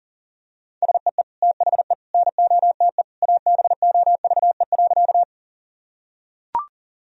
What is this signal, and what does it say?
Morse code, heard as a keyed tone, about 40 wpm, 700 Hz. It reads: SEE THE NOTE ABOVE.
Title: Sentences from Top 1000 Words - Review Part 2 - 40wpm